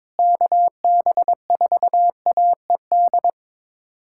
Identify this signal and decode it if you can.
Morse code, heard as a keyed tone, about 21 wpm, 700 Hz. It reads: KB4AED
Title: USA Callsign DE Callsign - 6 Characters - 22wpm